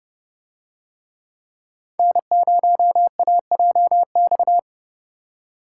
Morse code, heard as a keyed tone, about 28 wpm, 700 Hz. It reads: N0AJX